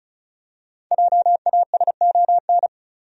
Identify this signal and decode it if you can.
Morse code, heard as a keyed tone, about 36 wpm, 700 Hz. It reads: JASON
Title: USA Common Names - Max 5 Letters - 35wpm